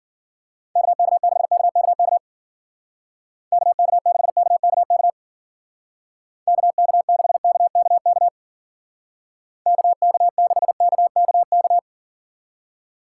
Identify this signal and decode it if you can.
Morse code, heard as a keyed tone, about 50 wpm, 700 Hz. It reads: KK6KKK KK6KKK KK6KKK KK6KKK